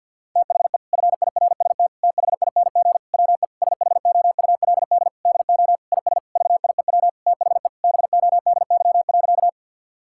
Morse code, heard as a keyed tone, about 50 wpm, 700 Hz. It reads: THE FIRST THING WE SHOULD DO IS VIEW THE BODY.
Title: Sentences from Top 1000 Words - Review Part 6 - 50wpm